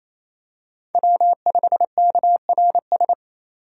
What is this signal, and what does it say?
Morse code, heard as a keyed tone, about 28 wpm, 700 Hz. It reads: W5KRS
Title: USA Call Signs - 5 Characters - 28wpm